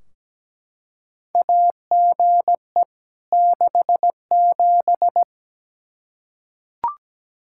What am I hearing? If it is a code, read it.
Morse code, heard as a keyed tone, about 17 wpm, 700 Hz. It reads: AGE 67
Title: K1USN SST QSO Practice 17wpm